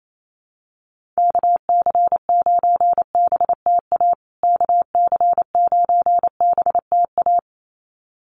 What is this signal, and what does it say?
Morse code, heard as a keyed tone, about 28 wpm, 700 Hz. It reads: KC9BTA KC9BTA
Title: USA Call Signs Repeated 2x 28wpm